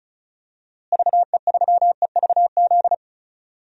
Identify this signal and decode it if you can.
Morse code, heard as a keyed tone, about 34 wpm, 700 Hz. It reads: VE3EVZ
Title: DXCC-100 Call Signs - 6 Characters - Repeated 3x - 35wpm